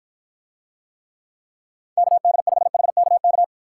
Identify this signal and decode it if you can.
Morse code, heard as a keyed tone, about 53 wpm, 700 Hz. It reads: KD5HKK